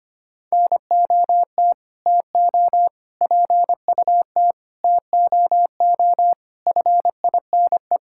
Morse code, heard as a keyed tone, about 25 wpm, 700 Hz. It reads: NOT TO PUT TOO FINE
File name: practice / 25wpm / Sentences with 9-Word Phrases - 25wpm